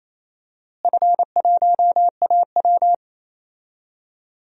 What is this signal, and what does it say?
Morse code, heard as a keyed tone, about 28 wpm, 700 Hz. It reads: F1AW
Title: DXCC-100 Call Signs - 4 Characters - Repeated 2x - 28wpm